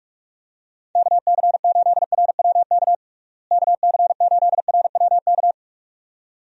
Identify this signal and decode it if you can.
Morse code, heard as a keyed tone, about 44 wpm, 700 Hz. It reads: KC8RWK KC8RWK
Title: USA Call Signs - 6 Characters - Repeated 2x - 45wpm